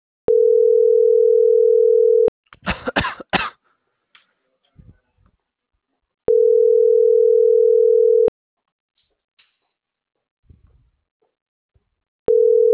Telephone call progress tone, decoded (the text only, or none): ringback tone